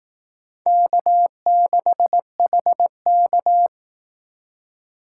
Morse code, heard as a keyed tone, about 19 wpm, 700 Hz. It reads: K6HK
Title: Speed Racing - USA Call Signs 15wpm